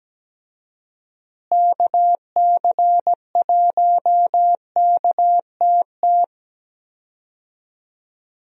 Morse code, heard as a keyed tone, about 17 wpm, 700 Hz. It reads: KC1KTT
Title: USA Call Signs - 6 Characters - Repeated 3x - 17wpm